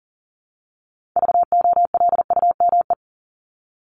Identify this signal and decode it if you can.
Morse code, heard as a keyed tone, about 40 wpm, 700 Hz. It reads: VOLUME
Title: Top 1751-2000 Words 40wpm